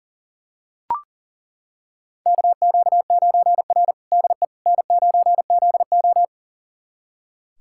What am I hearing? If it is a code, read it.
Morse code, heard as a keyed tone, about 40 wpm, 700 Hz. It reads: KQ9R DE N9ZO
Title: USA Callsign DE Callsign - 4 Characters - 40wpm